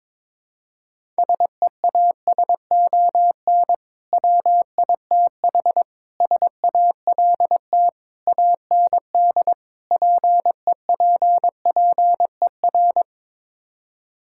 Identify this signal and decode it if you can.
Morse code, heard as a keyed tone, about 22 wpm, 700 Hz. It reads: SEASON WITH SALT AND PEPPER
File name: practice / 22wpm / Build A Phrase - 5-Word Phrases - 22wpm